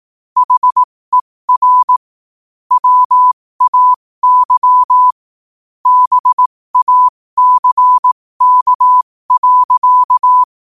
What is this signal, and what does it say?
Morse code, heard as a keyed tone, about 18 wpm, spaced as for 15 wpm, 1000 Hz. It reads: HER WAY BACK.
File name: qotdfiles / 15wpm